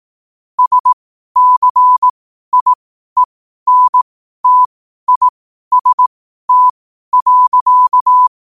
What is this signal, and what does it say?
Morse code, heard as a keyed tone, about 18 wpm, spaced as for 12 wpm, 1000 Hz. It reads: SCIENTIST.
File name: qotdfiles / 12wpm